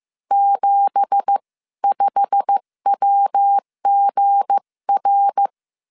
Morse code, heard as a keyed tone, about 15 wpm, spaced as for 13 wpm, 800 Hz. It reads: 75WGR
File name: CW / DET / 1 9 1500C S C 14wpm 3245